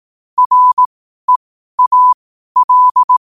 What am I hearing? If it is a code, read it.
Morse code, heard as a keyed tone, about 18 wpm, spaced as for 12 wpm, 1000 Hz. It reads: REAL
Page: qotd-at-12-wpm